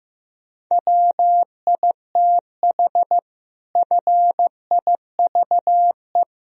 Morse code, heard as a keyed tone, about 16 wpm, 700 Hz. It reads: WITH FIVE
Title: Sentences from Top 1000 Words - Review Part 3 - 15wpm